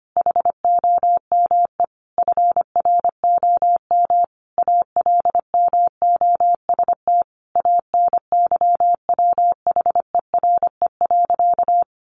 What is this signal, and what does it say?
Morse code, heard as a keyed tone, about 25 wpm, 700 Hz. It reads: HOME FROM ALMOST ANYWHERE.
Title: Sentences from Top 1400 Words 25wpm